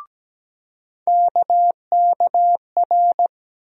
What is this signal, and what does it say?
Morse code, heard as a keyed tone, about 17 wpm, 700 Hz. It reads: KKR
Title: USA Call Sign Postfixes - 3 Letters 17wpm